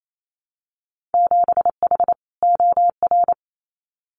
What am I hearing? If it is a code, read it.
Morse code, heard as a keyed tone, about 28 wpm, 700 Hz. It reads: 7H OR
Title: Winter Field Day Exchanges Encore 28wpm